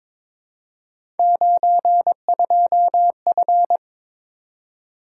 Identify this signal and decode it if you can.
Morse code, heard as a keyed tone, about 22 wpm, 700 Hz. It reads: 92F
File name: practice / 22wpm / Three Letter-Number Combinations - 22wpm